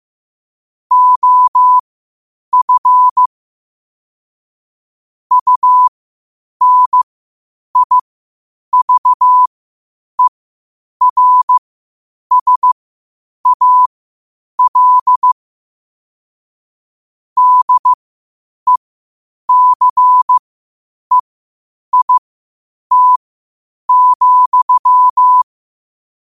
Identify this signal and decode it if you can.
Morse code, heard as a keyed tone, about 15 wpm, spaced as for 8 wpm, 1000 Hz. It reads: OF UNIVERSAL DECEIT,
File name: qotdfiles / 8wpm